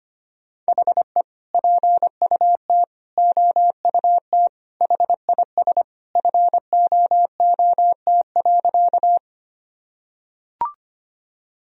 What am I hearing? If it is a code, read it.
Morse code, heard as a keyed tone, about 25 wpm, 700 Hz. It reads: HE PUT OUT HIS FOOT.
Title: Sentences from Top 1000 Words - Review Part 3 - 25wpm